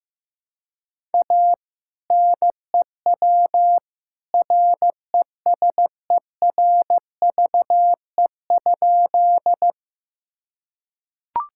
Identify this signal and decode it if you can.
Morse code, heard as a keyed tone, about 15 wpm, 700 Hz. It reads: A NEW RESERVE?
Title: Sentences from Top 1000 Words - Review Part 6 - 15wpm